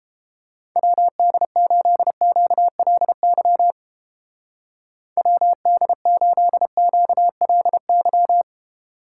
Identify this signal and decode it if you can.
Morse code, heard as a keyed tone, about 31 wpm, 700 Hz. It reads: WD8QLY WD8QLY